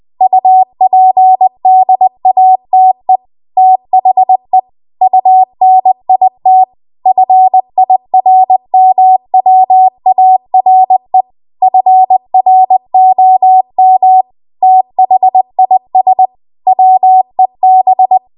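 Morse code, heard as a keyed tone, about 20 wpm, 750 Hz. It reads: UPDATE THE UNIT FIRMWARE FROM THIS WEB
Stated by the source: Morse code practice at 20 WPM.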